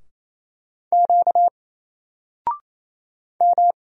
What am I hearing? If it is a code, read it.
Morse code, heard as a keyed tone, about 28 wpm, 700 Hz. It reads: Q M
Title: Single Character (Letter:Number:Punctuation:Prosign) 28wpm